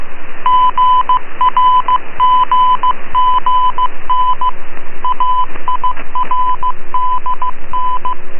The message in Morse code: GRGGN AIRDN